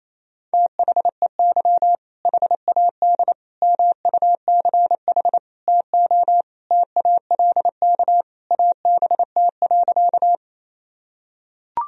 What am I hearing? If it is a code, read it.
Morse code, heard as a keyed tone, about 28 wpm, 700 Hz. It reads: THEY HAD MUCH TO TALK ABT.
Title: Sentences from Top 400 Words and Abbreviations 28wpm